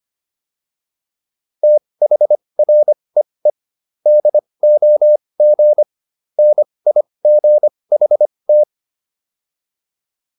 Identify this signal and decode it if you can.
Morse code, heard as a keyed tone, about 25 wpm, spaced as for 20 wpm, 600 Hz. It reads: THREE DOG NIGHT